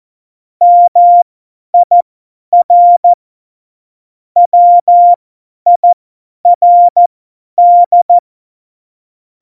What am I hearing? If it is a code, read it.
Morse code, heard as a keyed tone, about 14 wpm, spaced as for 10 wpm, 700 Hz. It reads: MIR WIRD
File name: Morsekurs / Witze / Witz 13. August 2019, 50 F70